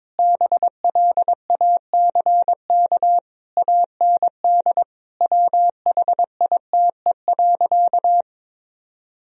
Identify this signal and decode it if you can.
Morse code, heard as a keyed tone, about 22 wpm, 700 Hz. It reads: BLACK AND WHITE.
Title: Sentences from Top 1000 Words - Review Part 2 - 22wpm